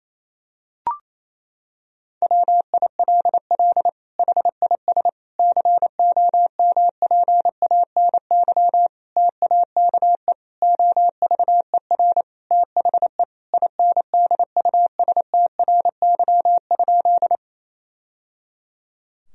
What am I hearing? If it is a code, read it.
Morse code, heard as a keyed tone, about 28 wpm, 700 Hz. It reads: WILL HIS COMPANY TAKE OVER THE INDUSTRY?
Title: Sentences from Top 1000 Words - Review Part 4 - 28wpm